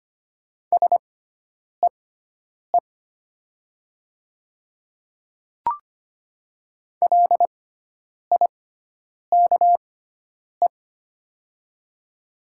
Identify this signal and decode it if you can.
Morse code, heard as a keyed tone, about 25 wpm, spaced as for 9 wpm, 700 Hz. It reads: SEE LIKE
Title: 6x Character Spacing - Top 100 Words - 25wpm